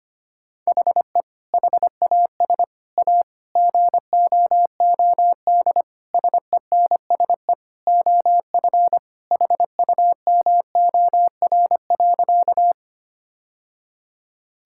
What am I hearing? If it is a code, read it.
Morse code, heard as a keyed tone, about 25 wpm, 700 Hz. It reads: HE HAS A GOOD SENSE OF HUMOR.
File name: practice / 25wpm / Sentences with 2-Word Phrases - 25wpm